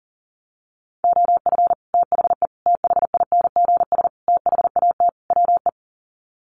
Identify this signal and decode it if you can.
Morse code, heard as a keyed tone, about 40 wpm, 700 Hz. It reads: OF THE THINGS THAT WE
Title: Sets of 5 Words Encore 40wpm